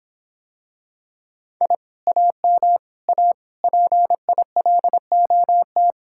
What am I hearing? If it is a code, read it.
Morse code, heard as a keyed tone, about 26 wpm, 700 Hz. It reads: I AM A PILOT